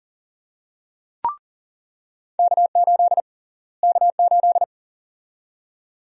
Morse code, heard as a keyed tone, about 37 wpm, 700 Hz. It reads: K8 K8
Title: USA Call Sign Prefixes - Repeated 2x - 40wpm